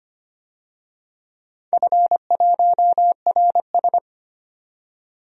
Morse code, heard as a keyed tone, about 25 wpm, 700 Hz. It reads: F1RS
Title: DXCC-100 Call Signs - 4 Characters - 25wpm